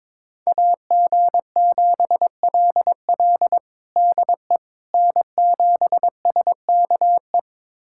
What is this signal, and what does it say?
Morse code, heard as a keyed tone, about 22 wpm, 700 Hz. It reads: AG7LL DE N7SKE
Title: USA Callsign DE Callsign - 5 Characters - 22wpm